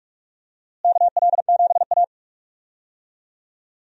Morse code, heard as a keyed tone, about 43 wpm, 700 Hz. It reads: KL7A
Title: DXCC-100 Call Signs - 4 Characters - Repeated 3x - 45wpm